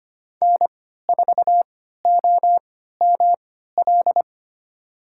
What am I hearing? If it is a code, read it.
Morse code, heard as a keyed tone, about 25 wpm, spaced as for 14 wpm, 700 Hz. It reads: N4OML
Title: USA Call Signs - 5 Characters - 3x Character Spacing - 25wpm